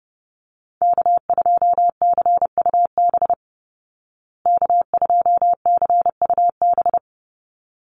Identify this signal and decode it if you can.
Morse code, heard as a keyed tone, about 30 wpm, 700 Hz. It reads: K2CUB K2CUB
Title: Call Signs Repeated 3x - 30wpm